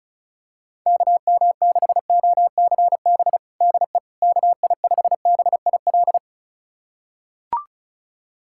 Morse code, heard as a keyed tone, about 36 wpm, 700 Hz. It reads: KM6OCB DE KI5BIL
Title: USA Callsign DE Callsign - 6 Characters - 35wpm